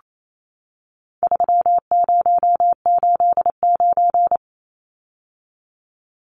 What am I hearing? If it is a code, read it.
Morse code, heard as a keyed tone, about 28 wpm, 700 Hz. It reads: 3089
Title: Numbers - 4-Digit 28wpm